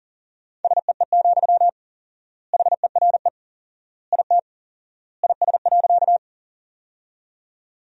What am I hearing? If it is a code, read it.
Morse code, heard as a keyed tone, about 40 wpm, 700 Hz. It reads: SEE, HERE IT IS.
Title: 4x Word-Spacing - Sentences from Top 200 Words - 40wpm